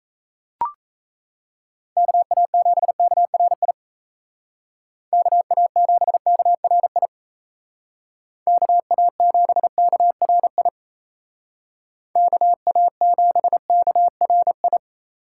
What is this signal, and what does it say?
Morse code, heard as a keyed tone, about 36 wpm, 700 Hz. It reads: KA7KRI KA7KRI KA7KRI KA7KRI